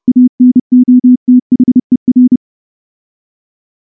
Morse code, heard as a keyed tone, about 28 wpm, 250 Hz. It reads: ANOTHER
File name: morse / morse-training-30WPM